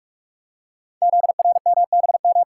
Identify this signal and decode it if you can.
Morse code, heard as a keyed tone, about 44 wpm, 690 Hz. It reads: ZRMBM
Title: Code Groups - 5 Letters - 45wpm